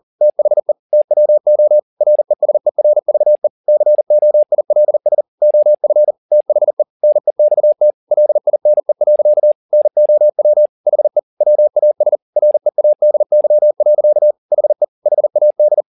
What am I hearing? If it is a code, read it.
Morse code, heard as a keyed tone, about 40 wpm, 600 Hz. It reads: THE TWO RESERVE COILS OF THE NEXT LINE. NOW HE WAS READY. HE HAD